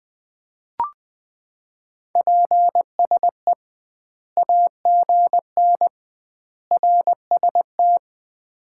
Morse code, heard as a keyed tone, about 20 wpm, 700 Hz. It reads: PSE AGN RST
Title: Common QSO Phrases - 2x Word-Spacing - 20wpm